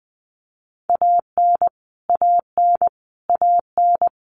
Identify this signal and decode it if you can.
Morse code, heard as a keyed tone, about 20 wpm, 700 Hz. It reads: AN AN AN
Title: Top 500 Words Repeated 20wpm